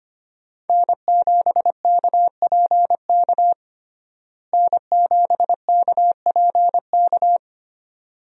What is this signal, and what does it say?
Morse code, heard as a keyed tone, about 25 wpm, 700 Hz. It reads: N7KPK N7KPK
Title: USA Call Signs - 5 Characters - Repeated 2x - 25wpm